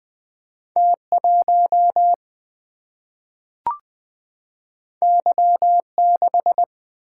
Morse code, heard as a keyed tone, about 20 wpm, 700 Hz. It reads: T 1 Y 6
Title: Single-Letter Single-Number 20wpm